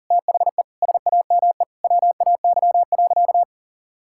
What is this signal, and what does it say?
Morse code, heard as a keyed tone, about 40 wpm, 700 Hz. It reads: THE SAME WAY.
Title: Sentences from Top 1600 Words 40wpm